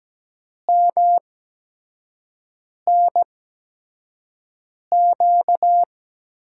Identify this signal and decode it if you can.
Morse code, heard as a keyed tone, about 17 wpm, spaced as for 5 wpm, 700 Hz. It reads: MNQ
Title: USA Call Signs - 8x Character Spacing - 17wpm